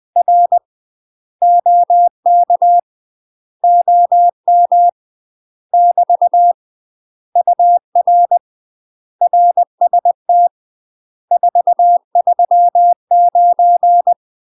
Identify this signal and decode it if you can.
Morse code, heard as a keyed tone, about 20 wpm, 700 Hz. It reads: R OK OM <BT> UR RST 439
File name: UK MASTER QSOs TRAINING / 20WPM / UK MASTER QSOs 0068